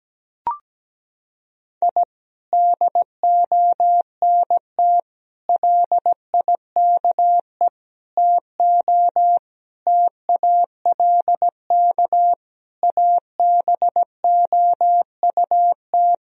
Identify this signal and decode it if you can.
Morse code, heard as a keyed tone, about 17 wpm, 700 Hz. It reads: I DONT LIKE TO TALK ABOUT
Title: Sentences from Top 1000 Words - Review Part 1 - 17wpm